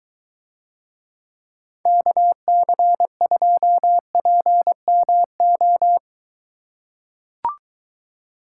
Morse code, heard as a keyed tone, about 22 wpm, 700 Hz. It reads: KC2PMO